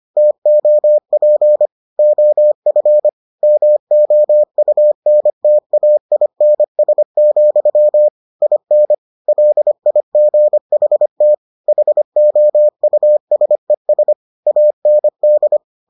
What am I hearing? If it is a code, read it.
Morse code, heard as a keyed tone, about 25 wpm, 600 Hz. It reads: TOP OF MOUNTAINS, IN LIGHT HOUSES AND